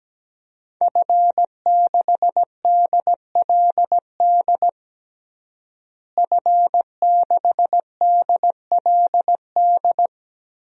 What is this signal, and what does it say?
Morse code, heard as a keyed tone, about 17 wpm, 700 Hz. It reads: F6DLD F6DLD
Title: DXCC-100 Call Signs - 5 Characters - Repeated 3x - 17wpm